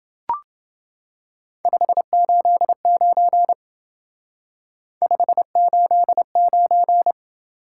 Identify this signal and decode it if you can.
Morse code, heard as a keyed tone, about 28 wpm, 700 Hz. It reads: 589 589